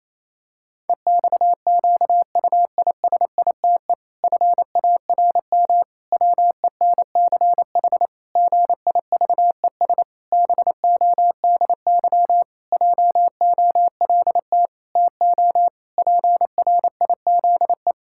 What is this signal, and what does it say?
Morse code, heard as a keyed tone, about 28 wpm, 700 Hz. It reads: EXQUISITE FARM WENCH GIVES BODY JOLT TO PRIZE
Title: Pangram Challenge v2 - 28wpm